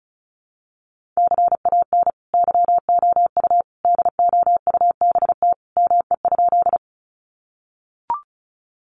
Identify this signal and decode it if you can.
Morse code, heard as a keyed tone, about 34 wpm, 700 Hz. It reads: CAN YOU DOUBT ME?
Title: Sentences from Top 1500 Words 35wpm